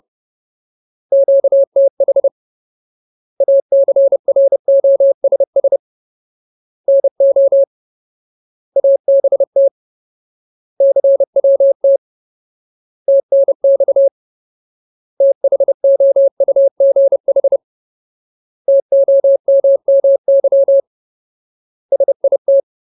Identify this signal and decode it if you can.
Morse code, heard as a keyed tone, about 30 wpm, 550 Hz. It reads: QTH ACROSS NO ABT CWT TNX THOUGH TOMMY SIT
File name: Ultimate Word List / 30WPM Ultimate